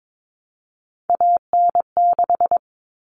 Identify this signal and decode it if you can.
Morse code, heard as a keyed tone, about 22 wpm, 700 Hz. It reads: AN6
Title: Common Cut Numbers - 3-Digit 22wpm